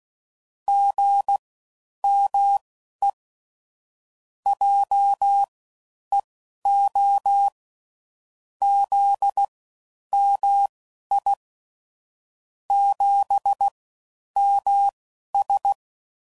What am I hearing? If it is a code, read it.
Morse code, heard as a keyed tone, about 16 wpm, spaced as for 10 wpm, 800 Hz. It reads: GME JEO ZMI 7MS
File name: Mixed / CW Comparisons / CW Comparisons Broken Characters